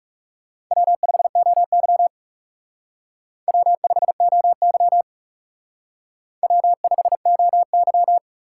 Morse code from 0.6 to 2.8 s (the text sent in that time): W5OY